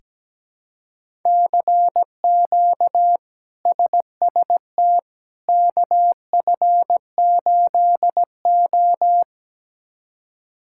Morse code, heard as a keyed tone, about 17 wpm, 700 Hz. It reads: CQ SST KF8O
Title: K1USN SST QSO Practice - 17wpm